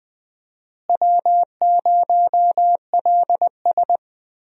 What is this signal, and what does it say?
Morse code, heard as a keyed tone, about 21 wpm, 700 Hz. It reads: W0LS
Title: USA Call Signs - 4 Characters - Repeated 2x - 20wpm